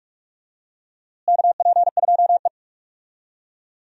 Morse code, heard as a keyed tone, about 42 wpm, 690 Hz. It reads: KP2E
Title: DXCC-100 Call Signs - 4 Characters - Repeated 3x - 45wpm